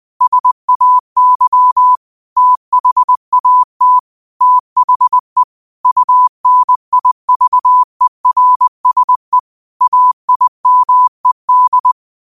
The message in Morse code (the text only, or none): SAY THAT THE UNIVERSE AIMED